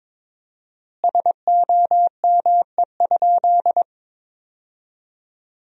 Morse code, heard as a keyed tone, about 22 wpm, 700 Hz. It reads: SOME?
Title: 4x Word-Spacing - Sentences from Top 100 Words - 22wpm